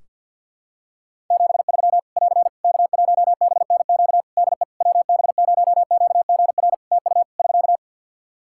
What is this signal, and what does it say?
Morse code, heard as a keyed tone, about 51 wpm, 700 Hz. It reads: 73 <AR> K1BNQ DE WB0QGR TU <SK>
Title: Practice QSO - Exchange 4 - Ending - 50wpm